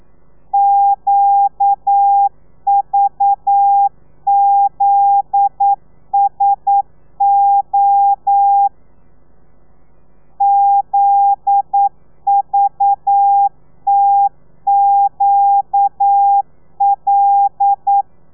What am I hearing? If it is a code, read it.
Morse code, heard as a keyed tone, about 9 wpm, 800 Hz. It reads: QVZSO ZVTQL
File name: telegrafia / i2tfj / Lez9 40